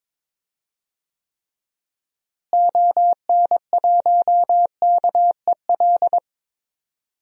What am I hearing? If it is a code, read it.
Morse code, heard as a keyed tone, about 22 wpm, 700 Hz. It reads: ON1KEL
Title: DXCC-100 Call Signs - 6 Characters - Repeated 3x - 22wpm